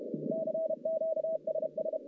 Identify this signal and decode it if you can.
Morse code, heard as a keyed tone, about 31 wpm, 640 Hz, with noise. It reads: CQ SS
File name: contests / CW-SS-k5tr- -1200